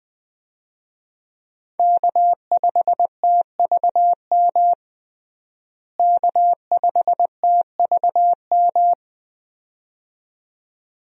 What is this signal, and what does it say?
Morse code, heard as a keyed tone, about 20 wpm, 700 Hz. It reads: K5TVM K5TVM
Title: USA Call Signs - 5 Characters - Repeated 2x - 20wpm